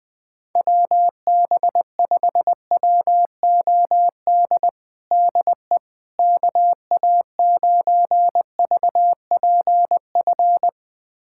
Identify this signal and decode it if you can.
Morse code, heard as a keyed tone, about 21 wpm, 700 Hz. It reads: WB5WOD DE KA9VPF
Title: USA Callsign DE Callsign - 6 Characters - 20wpm